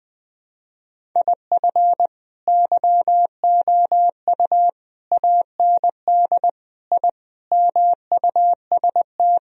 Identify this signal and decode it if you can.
Morse code, heard as a keyed tone, about 20 wpm, 700 Hz. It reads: IF YOU AND I MUST
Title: Sentences from Top 1000 Words - Review Part 1 - 20wpm